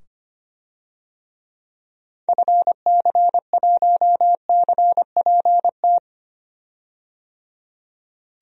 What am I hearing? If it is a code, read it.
Morse code, heard as a keyed tone, about 25 wpm, 700 Hz. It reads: FC1CPT
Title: DXCC-100 Call Signs - 6 Characters - 25wpm